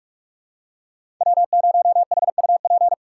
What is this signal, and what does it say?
Morse code, heard as a keyed tone, about 47 wpm, 700 Hz. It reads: W0HUP
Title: USA Call Signs - 3x Character Spacing - 45wpm